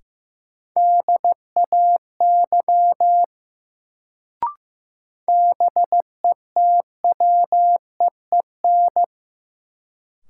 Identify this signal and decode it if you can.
Morse code, heard as a keyed tone, about 15 wpm, 700 Hz. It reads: DAY BETWEEN
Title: Sets of 2 Words - New Word Drill - 15wpm